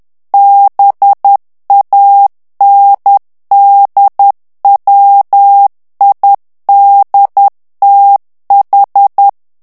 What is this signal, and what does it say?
Morse code, heard as a keyed tone, about 10 wpm, 800 Hz. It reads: BANDWIDTH
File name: ANGLAIS texte clair 3 vitesse 10 mots minute